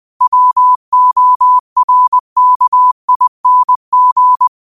Morse code, heard as a keyed tone, about 20 wpm, 1000 Hz. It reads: WORKING